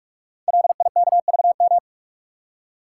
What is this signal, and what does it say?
Morse code, heard as a keyed tone, about 45 wpm, 690 Hz. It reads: LIKVM
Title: Code Groups - 5 Letters - 45wpm